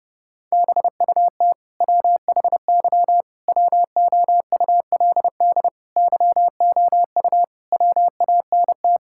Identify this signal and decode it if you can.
Morse code, heard as a keyed tone, about 28 wpm, 700 Hz. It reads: BUT WHY WOULD YOU WANT
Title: Sentences from Top 1000 Words - Review Part 6 - 30wpm